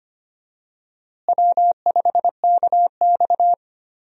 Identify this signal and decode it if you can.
Morse code, heard as a keyed tone, about 25 wpm, 700 Hz. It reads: W5KX
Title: USA Call Signs - 4 Characters - Repeated 3x - 25wpm